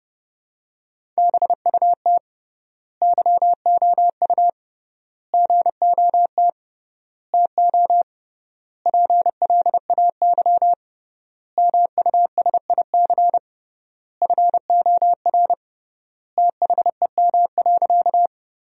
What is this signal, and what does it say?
Morse code, heard as a keyed tone, about 30 wpm, 700 Hz. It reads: BUT YOU GOT TO PLAY MUSIC FOR THEM.
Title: 3x Word-Spacing - Sentences from Top 300 Words - 30wpm